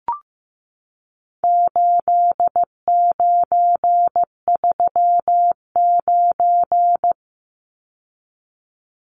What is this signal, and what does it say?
Morse code, heard as a keyed tone, about 15 wpm, 700 Hz. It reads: 8939
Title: Numbers - 4-Digit 15wpm